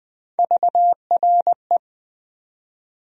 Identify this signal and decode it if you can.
Morse code, heard as a keyed tone, about 21 wpm, 700 Hz. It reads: VRE
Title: USA Call Sign Postfixes - 3 Letters 20wpm